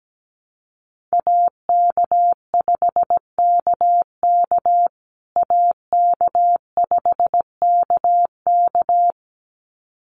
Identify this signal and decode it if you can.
Morse code, heard as a keyed tone, about 17 wpm, 700 Hz. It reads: AK5KK AK5KK
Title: USA Call Signs Repeated 2x 17wpm